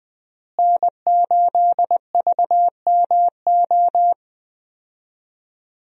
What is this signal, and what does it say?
Morse code, heard as a keyed tone, about 20 wpm, 700 Hz. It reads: N8VMO
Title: USA Call Signs - 5 Characters - Repeated 3x - 20wpm